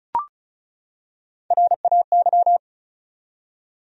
Morse code, heard as a keyed tone, about 35 wpm, 700 Hz. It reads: RAY
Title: USA Common Names - Max 4 Letters - 35wpm